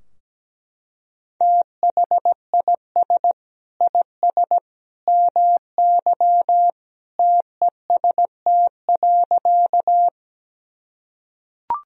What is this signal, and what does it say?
Morse code, heard as a keyed tone, about 17 wpm, 700 Hz. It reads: THIS IS MY TEST.
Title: Sentences from Top 1000 Words - Review Part 3 - 17wpm